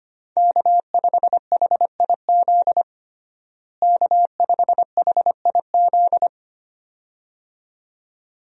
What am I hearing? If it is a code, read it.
Morse code, heard as a keyed tone, about 25 wpm, 700 Hz. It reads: K5HIZ K5HIZ
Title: USA Call Signs - 5 Characters - Repeated 2x - 25wpm